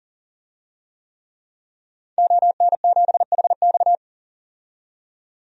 Morse code, heard as a keyed tone, about 40 wpm, 700 Hz. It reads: ON7HX
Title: DXCC-100 Call Signs - 5 Characters - 40wpm